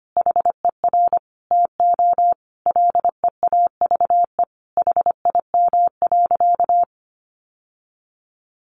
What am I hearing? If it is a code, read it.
Morse code, heard as a keyed tone, about 25 wpm, 700 Hz. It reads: HER TO LEAVE HIM.
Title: Sentences from Top 1500 Words 25wpm